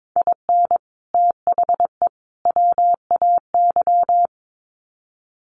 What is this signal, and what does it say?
Morse code, heard as a keyed tone, about 22 wpm, 700 Hz. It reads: IN THE WAY
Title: Sets of 3 Words 22wpm